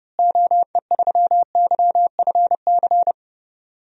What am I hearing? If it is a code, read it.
Morse code, heard as a keyed tone, about 30 wpm, 700 Hz. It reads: OE3YFC
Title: DXCC-100 Call Signs - 6 Characters - Repeated 2x - 30wpm